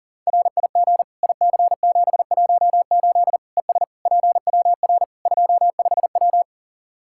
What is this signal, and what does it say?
Morse code, heard as a keyed tone, about 40 wpm, 700 Hz. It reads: RIG IC718 ES PWR 25W
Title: Practice QSO - Exchange 2 - Info - 40wpm